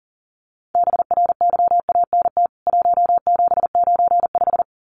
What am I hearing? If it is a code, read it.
Morse code, heard as a keyed tone, about 40 wpm, 700 Hz. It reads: BRYANT 1795
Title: CWT Exchange 40wpm